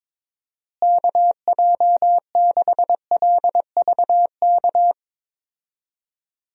Morse code, heard as a keyed tone, about 22 wpm, 700 Hz. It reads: KJ6LVK
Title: USA Call Signs - 6 Characters - Repeated 3x - 22wpm